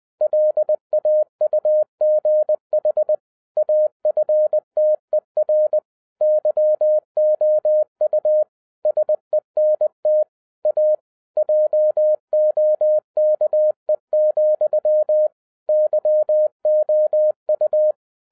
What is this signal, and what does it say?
Morse code, heard as a keyed tone, about 20 wpm, 610 Hz. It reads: LAUGH AFTER YOU SENT A JOKE, YOU
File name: 2011-10-mp3 / n13-TIPS FROM MY 34 YEARS OF EXPERIENCE 20wpm